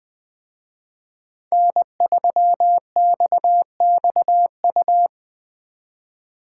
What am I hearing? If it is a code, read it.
Morse code, heard as a keyed tone, about 23 wpm, 700 Hz. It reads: N3XXU
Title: Speed Racing - USA 1x3 Call Signs 20wpm